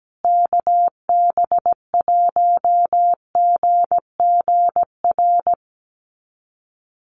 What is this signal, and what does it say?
Morse code, heard as a keyed tone, about 17 wpm, 700 Hz. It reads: KB1GGR
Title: USA Call Signs Repeated 3x 17wpm